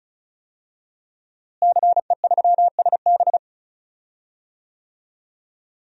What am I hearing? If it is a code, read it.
Morse code, heard as a keyed tone, about 34 wpm, 700 Hz. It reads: CE3SB
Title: DXCC-100 Call Signs - 5 Characters - Repeated 2x - 35wpm